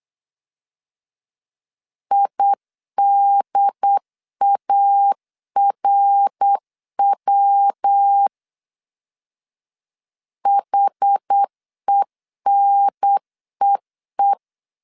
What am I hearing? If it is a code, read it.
Morse code, heard as a keyed tone, about 8 wpm, 800 Hz. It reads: IDARW HENEE
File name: CW / DET / 1 3 1500C S A 8wpm 4229